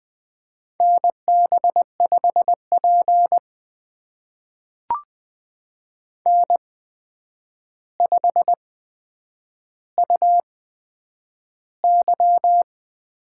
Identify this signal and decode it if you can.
Morse code, heard as a keyed tone, about 20 wpm, spaced as for 9 wpm, 700 Hz. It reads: NB5P N5UY
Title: USA Call Signs - 4 Characters - 8x Character Spacing - 20wpm